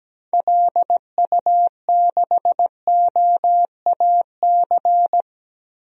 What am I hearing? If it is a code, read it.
Morse code, heard as a keyed tone, about 17 wpm, 700 Hz. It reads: LU6OAC
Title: DXCC-100 Call Signs - 6 Characters - Repeated 2x - 17wpm